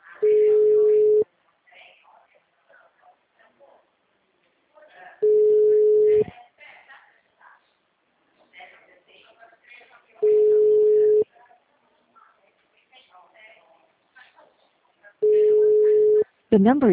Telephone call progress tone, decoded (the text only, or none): ringback tone